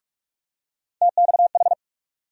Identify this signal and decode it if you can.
Morse code, heard as a keyed tone, about 46 wpm, 700 Hz. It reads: TXH
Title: USA Call Sign Postfixes - 3 Letters 45wpm